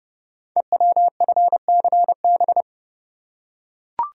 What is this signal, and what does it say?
Morse code, heard as a keyed tone, about 28 wpm, 700 Hz. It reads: EWFCB
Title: Code Groups - 5 Letters - 30wpm